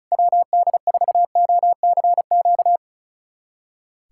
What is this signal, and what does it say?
Morse code, heard as a keyed tone, about 36 wpm, 700 Hz. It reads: WD4OCQ
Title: USA Callsign DE Callsign 35wpm